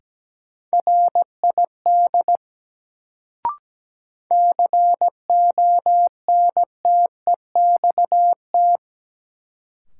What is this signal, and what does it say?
Morse code, heard as a keyed tone, about 17 wpm, 700 Hz. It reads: RID CONTEXT
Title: Sets of 4 Words - New Word Drill - 17wpm